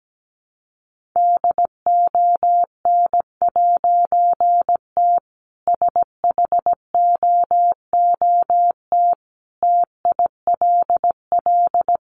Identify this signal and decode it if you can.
Morse code, heard as a keyed tone, about 17 wpm, 700 Hz. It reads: DON'T SHOOT TILL
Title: A Sound Of Thunder 17wpm